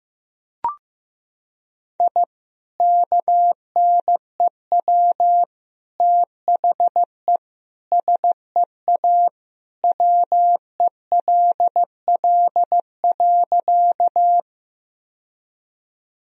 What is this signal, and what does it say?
Morse code, heard as a keyed tone, about 15 wpm, 700 Hz. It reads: I KNEW THE SEA WELL.
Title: Sentences from Top 1000 Words - Review Part 1 - 15wpm